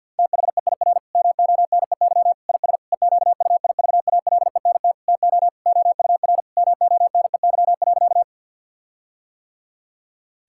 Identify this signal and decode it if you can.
Morse code, heard as a keyed tone, about 50 wpm, 700 Hz. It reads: THEIR MONEY IS EQUIVALENT TO OUR MONEY.
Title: Sentences from Top 1600 Words 50wpm